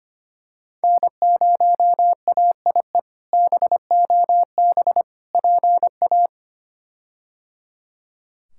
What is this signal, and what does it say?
Morse code, heard as a keyed tone, about 25 wpm, 700 Hz. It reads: N0AIE BOB PA
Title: K1USN SST QSO Practice - 25wpm